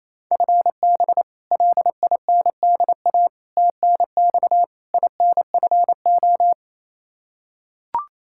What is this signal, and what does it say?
Morse code, heard as a keyed tone, about 28 wpm, 700 Hz. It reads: FB LINDA TNX INFO
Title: Practice QSO - Exchange 3 - More Info - 28wpm